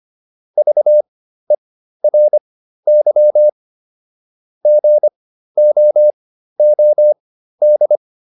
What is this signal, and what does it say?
Morse code, heard as a keyed tone, about 25 wpm, spaced as for 13 wpm, 600 Hz. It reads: VERY GOOD
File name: QSO 210 13